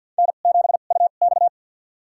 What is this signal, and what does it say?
Morse code, heard as a keyed tone, about 52 wpm, spaced as for 36 wpm, 700 Hz. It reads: N7UX
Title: USA Call Signs - 4 Characters - 2x Character Spacing - 50wpm